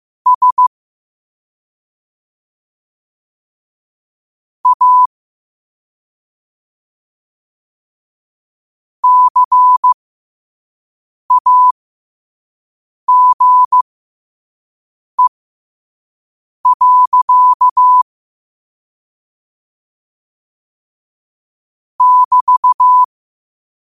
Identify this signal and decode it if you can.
Morse code, heard as a keyed tone, about 15 wpm, spaced as for 5 wpm, 1000 Hz. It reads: S A CAGE. <BT>
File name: qotdfiles / 5wpm